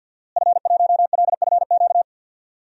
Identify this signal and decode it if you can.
Morse code, heard as a keyed tone, about 50 wpm, 700 Hz. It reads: F1LFQ
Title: DXCC-100 Call Signs - 5 Characters - Repeated 3x - 50wpm